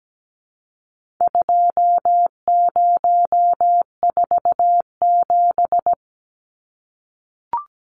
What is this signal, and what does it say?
Morse code, heard as a keyed tone, about 17 wpm, 700 Hz. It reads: 2047
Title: Numbers - 4-Digit 17wpm